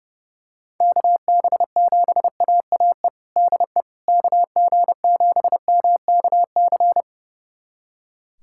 Morse code, heard as a keyed tone, about 32 wpm, 700 Hz. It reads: KB7AAE DE KG7MKC
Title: USA Callsign DE Callsign - 6 Characters - 30wpm